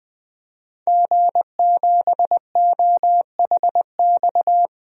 Morse code, heard as a keyed tone, about 20 wpm, 700 Hz. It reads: G7OHX
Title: DXCC-100 Call Signs - 5 Characters - 20wpm